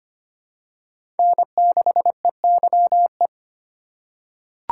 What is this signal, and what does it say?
Morse code, heard as a keyed tone, about 25 wpm, 700 Hz. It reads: N6EYE
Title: USA Call Signs - 5 Characters - 3x Character Spacing - 25wpm